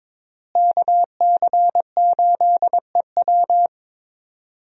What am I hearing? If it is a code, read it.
Morse code, heard as a keyed tone, about 22 wpm, 690 Hz. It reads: KC8EW
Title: USA Call Signs - 5 Characters - Repeated 3x - 22wpm